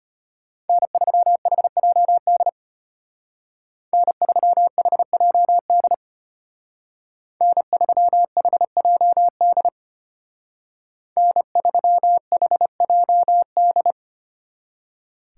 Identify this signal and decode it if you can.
Morse code, heard as a keyed tone, about 33 wpm, 700 Hz. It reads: N3HJD N3HJD N3HJD N3HJD